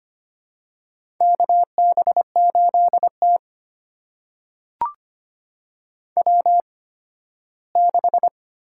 Morse code, heard as a keyed tone, about 25 wpm, spaced as for 15 wpm, 700 Hz. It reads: KB8T W6
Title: USA Call Signs - 4 Characters - 8x Character Spacing - 25wpm